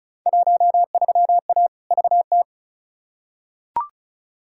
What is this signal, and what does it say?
Morse code, heard as a keyed tone, about 34 wpm, 700 Hz. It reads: 13A VT
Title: Field Day Exchanges Encore - 35wpm